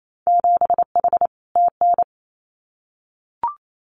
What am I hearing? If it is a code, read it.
Morse code, heard as a keyed tone, about 28 wpm, 700 Hz. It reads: 7H TN
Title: Winter Field Day Exchanges Encore 28wpm